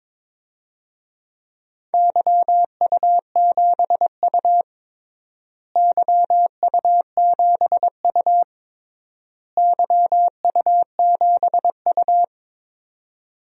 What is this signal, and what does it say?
Morse code, heard as a keyed tone, about 22 wpm, 700 Hz. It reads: YU7U YU7U YU7U
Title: DXCC-100 Call Signs - 4 Characters - Repeated 3x - 22wpm